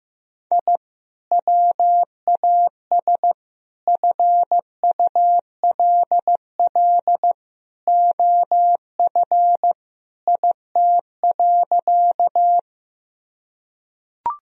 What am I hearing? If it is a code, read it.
Morse code, heard as a keyed tone, about 15 wpm, 700 Hz. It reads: I WAS FULL OF IT.
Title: Sentences from Top 1000 Words - Review Part 3 - 15wpm